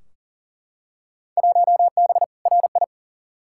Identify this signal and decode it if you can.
Morse code, heard as a keyed tone, about 39 wpm, 700 Hz. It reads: 1B RI
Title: Field Day Exchanges Encore - 40wpm